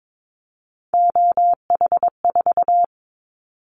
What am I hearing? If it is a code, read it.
Morse code, heard as a keyed tone, about 22 wpm, 700 Hz. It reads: OH4
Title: Callsign Prefixes for Top 50 DXCC - 22wpm